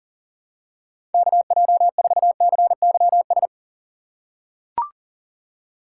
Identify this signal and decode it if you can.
Morse code, heard as a keyed tone, about 40 wpm, 700 Hz. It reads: KJ4CYS
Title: USA Call Signs - 6 Characters - 6x Character Spacing - 40wpm